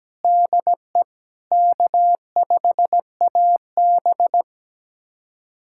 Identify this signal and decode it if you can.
Morse code, heard as a keyed tone, about 17 wpm, 700 Hz. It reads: DE K5AB
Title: USA Callsign DE Callsign 17wpm